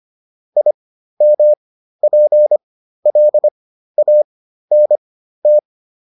Morse code, heard as a keyed tone, about 25 wpm, spaced as for 13 wpm, 600 Hz. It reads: IMPLANT
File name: IM7 13